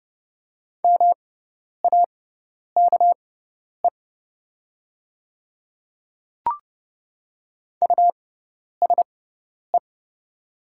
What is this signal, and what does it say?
Morse code, heard as a keyed tone, about 30 wpm, spaced as for 10 wpm, 700 Hz. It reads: MAKE USE
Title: 6x Character Spacing - Top 100 Words - 30wpm